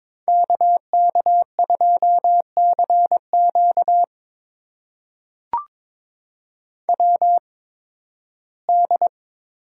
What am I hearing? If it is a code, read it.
Morse code, heard as a keyed tone, about 22 wpm, spaced as for 14 wpm, 700 Hz. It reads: KK2CQ WD
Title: USA Call Signs - 8x Character Spacing - 22wpm